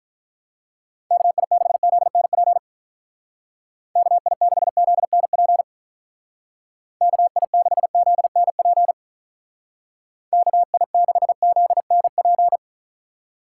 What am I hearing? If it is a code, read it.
Morse code, heard as a keyed tone, about 43 wpm, 700 Hz. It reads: KI6ZNP KI6ZNP KI6ZNP KI6ZNP
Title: Speed Racing - USA 2x3 Call Signs 35wpm